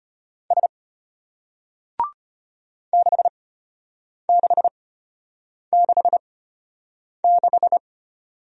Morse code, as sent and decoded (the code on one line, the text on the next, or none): ... / -.... -.... -.... -....
S 6666